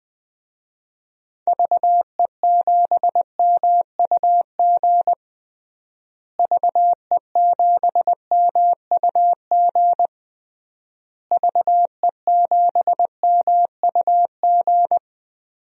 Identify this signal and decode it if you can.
Morse code, heard as a keyed tone, about 20 wpm, 700 Hz. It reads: VE7MUG VE7MUG VE7MUG
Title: DXCC-100 Call Signs - 6 Characters - Repeated 3x - 20wpm